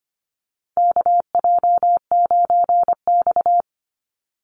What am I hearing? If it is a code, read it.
Morse code, heard as a keyed tone, about 25 wpm, 700 Hz. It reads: KJ9X
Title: NA SOTA Chaser Callsigns 25wpm